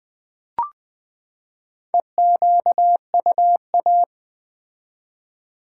Morse code, heard as a keyed tone, about 20 wpm, 700 Hz. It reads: EQUA
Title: Build A Word - 4 Letters At A Time - 8-Letter Words - 20wpm